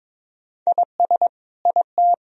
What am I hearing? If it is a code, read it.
Morse code, heard as a keyed tone, about 22 wpm, 700 Hz. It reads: IS IT
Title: Sentences from Top 1000 Words - Review Part 2 - 22wpm